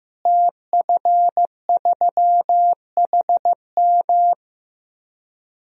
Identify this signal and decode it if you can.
Morse code, heard as a keyed tone, about 15 wpm, 700 Hz. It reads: TF3HM
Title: DXCC-100 Call Signs - 5 Characters - Repeated 2x - 15wpm